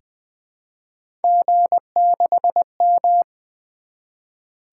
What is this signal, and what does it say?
Morse code, heard as a keyed tone, about 20 wpm, 700 Hz. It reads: G6M
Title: Three Letter-Number Combinations - 20wpm